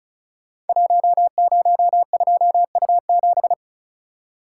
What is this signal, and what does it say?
Morse code, heard as a keyed tone, about 36 wpm, 700 Hz. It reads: 102U7
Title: Code Groups - 5 Letter:Numbers - 35wpm